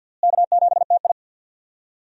Morse code, heard as a keyed tone, about 51 wpm, 700 Hz. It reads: K7TI
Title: USA Call Signs - 4 Characters - Repeated 3x - 50wpm